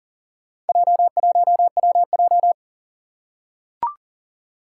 Morse code, heard as a keyed tone, about 39 wpm, 700 Hz. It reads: J1WJ
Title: Four Challenging Letter-Number Combinations 40wpm